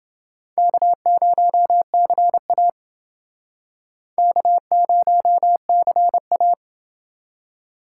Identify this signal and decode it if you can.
Morse code, heard as a keyed tone, about 30 wpm, 700 Hz. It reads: K0CA K0CA